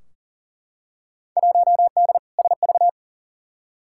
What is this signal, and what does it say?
Morse code, heard as a keyed tone, about 40 wpm, 700 Hz. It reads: 1D SV
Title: Field Day Exchanges - 40wpm